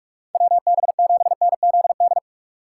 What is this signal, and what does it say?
Morse code, heard as a keyed tone, about 46 wpm, 700 Hz. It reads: WB7NZD
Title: USA Call Signs - 6 Characters - 3x Character Spacing - 45wpm